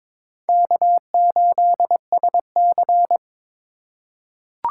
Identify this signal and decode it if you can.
Morse code, heard as a keyed tone, about 22 wpm, 700 Hz. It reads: K8SC
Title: USA Call Signs - 8x Character Spacing - 22wpm